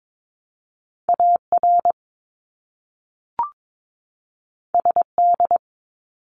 Morse code, heard as a keyed tone, about 22 wpm, 700 Hz. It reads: AR SD
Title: ARRL Sections 22wpm